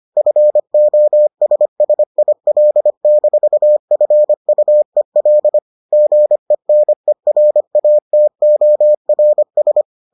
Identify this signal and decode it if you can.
Morse code, heard as a keyed tone, about 25 wpm, 600 Hz. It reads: FOSSIL-FUEL GENERATORS